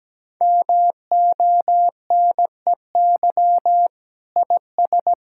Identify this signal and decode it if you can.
Morse code, heard as a keyed tone, about 17 wpm, 700 Hz. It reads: MONEY IS
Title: Sentences from Top 1600 Words 17wpm